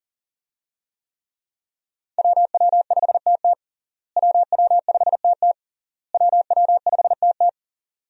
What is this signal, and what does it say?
Morse code, heard as a keyed tone, about 37 wpm, 700 Hz. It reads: WW5TT WW5TT WW5TT
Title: USA Call Signs - 5 Characters - Repeated 3x - 40wpm